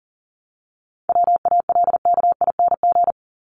Morse code, heard as a keyed tone, about 40 wpm, 700 Hz. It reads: WALKING
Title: ING Suffix - Top 100 Words with Base Words 40wpm